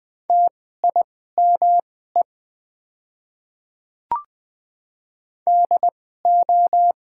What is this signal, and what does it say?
Morse code, heard as a keyed tone, about 20 wpm, spaced as for 14 wpm, 700 Hz. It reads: TIME DO
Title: 2x Character Spacing - Top 100 Words - 20wpm